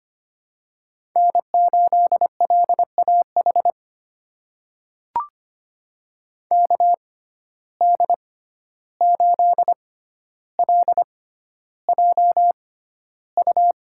Morse code, heard as a keyed tone, about 25 wpm, spaced as for 12 wpm, 700 Hz. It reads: N8LAH KD8LJU
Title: USA Call Signs - 6x Character Spacing - 25wpm